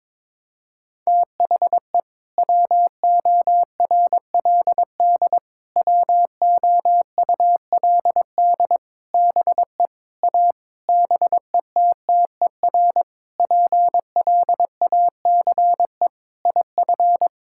Morse code, heard as a keyed tone, about 22 wpm, 700 Hz. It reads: THE WORLD WOULD BE A BETTER PLACE IF
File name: practice / 22wpm / Sentences with 8-Word Phrases - 22wpm